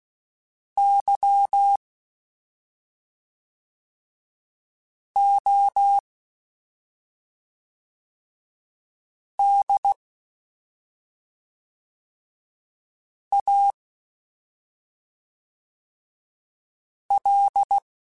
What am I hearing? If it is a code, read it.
Morse code, heard as a keyed tone, about 16 wpm, 800 Hz. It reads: Y O D A L
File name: Special INGs / Special INGs-mp3s / SlowToFaster ing @5